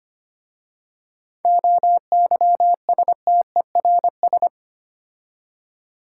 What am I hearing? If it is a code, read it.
Morse code, heard as a keyed tone, about 25 wpm, 700 Hz. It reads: OYSTERS
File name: practice / 25wpm / Difficult Words with Hints - 25wpm